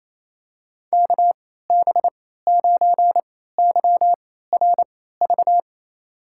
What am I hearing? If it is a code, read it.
Morse code, heard as a keyed tone, about 28 wpm, spaced as for 16 wpm, 700 Hz. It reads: KB9YRV
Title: USA Call Signs - 6 Characters - 3x Character Spacing - 28wpm